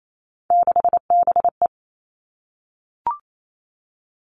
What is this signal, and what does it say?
Morse code, heard as a keyed tone, about 28 wpm, 700 Hz. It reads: 6BE
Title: Three Challenging Letter-Number Combinations 28wpm